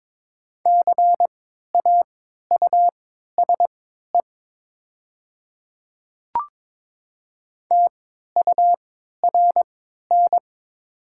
Morse code, heard as a keyed tone, about 22 wpm, spaced as for 12 wpm, 700 Hz. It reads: CAUSE TURN
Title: 3x Character Spacing - Top 300 Words - 22wpm